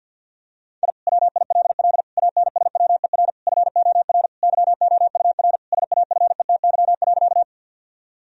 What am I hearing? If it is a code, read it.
Morse code, heard as a keyed tone, about 50 wpm, 700 Hz. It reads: I WILL ANSWER FOR YOUR SAFETY.
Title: Sentences from Top 1000 Words - Review Part 4 - 50wpm